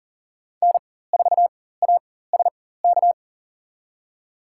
Morse code, heard as a keyed tone, about 40 wpm, spaced as for 19 wpm, 700 Hz. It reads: N4ASK
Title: USA Call Signs - 5 Characters - 4x Character Spacing - 40wpm